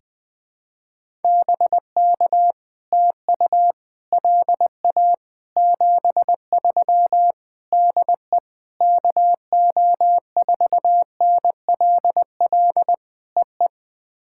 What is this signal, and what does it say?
Morse code, heard as a keyed tone, about 20 wpm, 700 Hz. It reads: BK TU LA 73 DE KO4NLL EE
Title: POTA QSO Practice - 20wpm